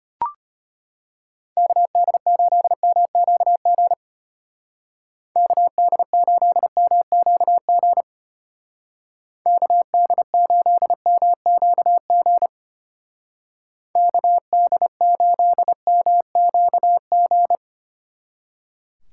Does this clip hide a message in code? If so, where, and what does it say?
Yes: Morse code from 1.6 to 17.6 s, heard as a keyed tone, about 32 wpm, 700 Hz. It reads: KD8MQG KD8MQG KD8MQG KD8MQG